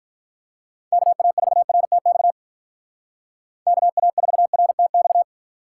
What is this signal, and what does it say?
Morse code, heard as a keyed tone, about 44 wpm, 700 Hz. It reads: KA4RTX KA4RTX
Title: Speed Racing - USA 2x3 Call Signs 35wpm